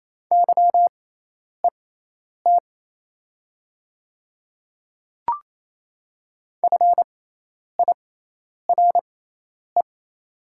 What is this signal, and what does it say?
Morse code, heard as a keyed tone, about 28 wpm, spaced as for 10 wpm, 700 Hz. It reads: YET FIRE
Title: 6x Character Spacing - Top 500 Words - 28wpm